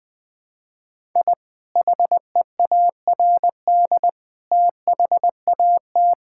Morse code, heard as a keyed tone, about 21 wpm, 700 Hz. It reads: I HEARD THAT
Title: Sentences with 3-Word Phrases - 20wpm